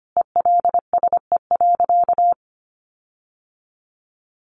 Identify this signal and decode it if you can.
Morse code, heard as a keyed tone, about 25 wpm, 700 Hz. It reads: ELSE.
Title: Sentences from Top 1500 Words 25wpm